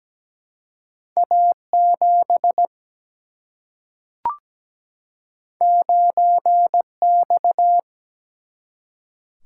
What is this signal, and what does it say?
Morse code, heard as a keyed tone, about 17 wpm, 700 Hz. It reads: A 7 9 X
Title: Single-Letter Single-Number 17wpm